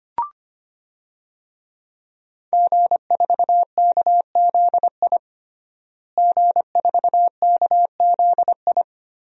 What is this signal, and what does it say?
Morse code, heard as a keyed tone, about 25 wpm, 700 Hz. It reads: G4KZI G4KZI
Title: DXCC-100 Call Signs - 5 Characters - Repeated 2x - 25wpm